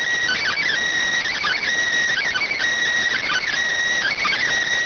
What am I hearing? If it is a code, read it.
Emergency Alert System (SAME) tones (header bursts), heard as harsh buzzing bursts. It is heard from 0.2 to 4.4 s.